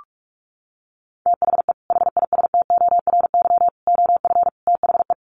Morse code, heard as a keyed tone, about 44 wpm, 700 Hz. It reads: THE HISTORY OF THE
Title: Sets of 4 Words 45wpm